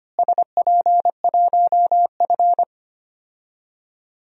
Morse code, heard as a keyed tone, about 25 wpm, 700 Hz. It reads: SP1F
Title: DXCC-100 Call Signs - 4 Characters - Repeated 2x - 25wpm